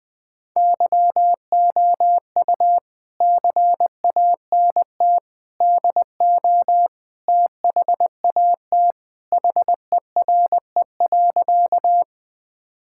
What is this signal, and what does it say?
Morse code, heard as a keyed tone, about 21 wpm, 700 Hz. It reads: YOU CANT DO THAT HERE.
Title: Sentences with 5-Word Phrases - 20wpm